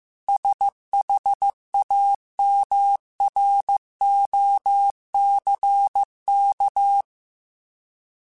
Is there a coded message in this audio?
Yes: Morse code from 0.3 to 7.0 s, heard as a keyed tone, about 15 wpm, 800 Hz. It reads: SHAMROCK